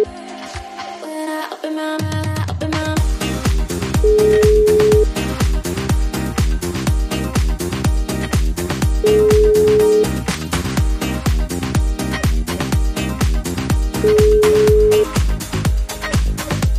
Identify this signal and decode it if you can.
Ringback tone (Europe, 425 Hz), with noise.